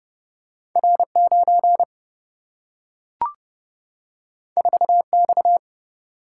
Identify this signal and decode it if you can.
Morse code, heard as a keyed tone, about 30 wpm, 700 Hz. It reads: R9 4X
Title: Single-Letter Single-Number 30wpm